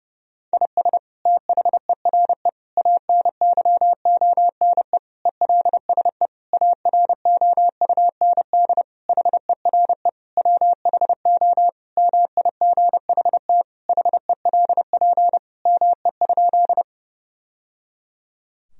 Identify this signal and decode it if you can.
Morse code, heard as a keyed tone, about 30 wpm, 700 Hz. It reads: IS THERE ANYONE ELSE AROUND HERE WHO MIGHT HELP ME?
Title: Sentences from Top 1000 Words - Review Part 3 - 30wpm